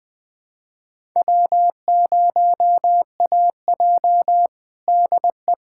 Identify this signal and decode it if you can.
Morse code, heard as a keyed tone, about 21 wpm, 700 Hz. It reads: W0AJ DE
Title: USA Callsign DE Callsign - 4 Characters - 20wpm